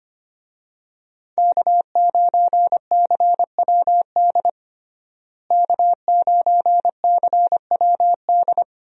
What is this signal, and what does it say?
Morse code, heard as a keyed tone, about 25 wpm, 700 Hz. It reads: K9CWD K9CWD
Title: USA Call Signs - 5 Characters - Repeated 2x - 25wpm